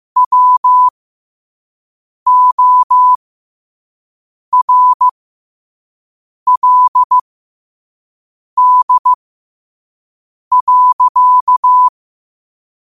Morse code, heard as a keyed tone, about 15 wpm, spaced as for 5 wpm, 1000 Hz. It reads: WORLD.
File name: qotdfiles / 5wpm